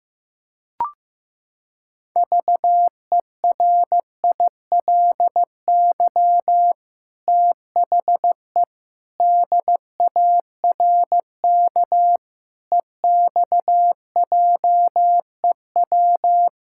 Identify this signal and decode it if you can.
Morse code, heard as a keyed tone, about 15 wpm, 700 Hz. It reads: VERILY THE DARK EXJEW
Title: Pangram Challenge v2 - 15wpm